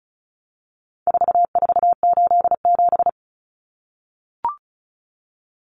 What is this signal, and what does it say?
Morse code, heard as a keyed tone, about 34 wpm, 700 Hz. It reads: 4487
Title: Numbers - 4-Digit 35wpm